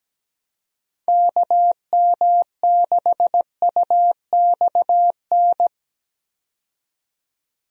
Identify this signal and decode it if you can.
Morse code, heard as a keyed tone, about 17 wpm, 700 Hz. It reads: KM6UXN
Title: USA Call Signs - 6 Characters - 17wpm